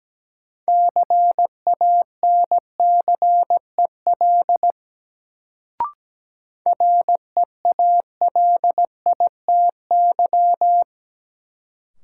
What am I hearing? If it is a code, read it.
Morse code, heard as a keyed tone, about 17 wpm, 700 Hz. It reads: CANCEL REALITY
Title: Sentences from Top 1200 Words - New Word Drill - 17wpm